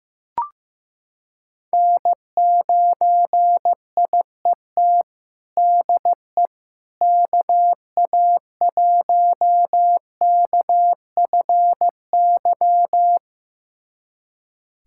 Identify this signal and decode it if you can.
Morse code, heard as a keyed tone, about 15 wpm, 700 Hz. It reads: N9IET DE KA1KFY
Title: USA Callsign DE Callsign 15wpm